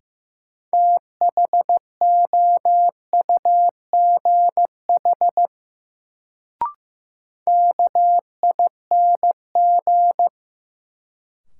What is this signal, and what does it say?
Morse code, heard as a keyed tone, about 16 wpm, 700 Hz. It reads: THOUGH KING
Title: Sentences from Top 400 Words - New Word Drill - 15wpm